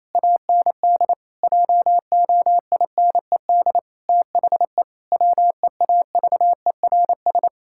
Morse code, heard as a keyed tone, about 28 wpm, 700 Hz. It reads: AND JOINED THE WEAVERS
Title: Pangram Challenge v2 - 28wpm